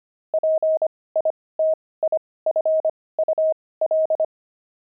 Morse code, heard as a keyed tone, about 25 wpm, spaced as for 18 wpm, 630 Hz. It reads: PITIFUL
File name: pangram-fs-18-cs-25-crazy